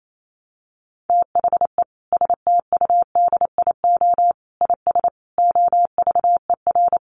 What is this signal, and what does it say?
Morse code, heard as a keyed tone, about 28 wpm, 700 Hz. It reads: THE STUDIO IS OVER
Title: Sentences from Top 1200 Words 28wpm